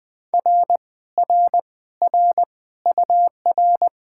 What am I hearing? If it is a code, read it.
Morse code, heard as a keyed tone, about 20 wpm, 700 Hz. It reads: R R R UR